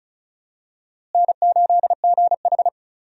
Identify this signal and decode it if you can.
Morse code, heard as a keyed tone, about 36 wpm, 700 Hz. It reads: N8GH
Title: USA Call Signs - 4 Characters - Repeated 3x - 35wpm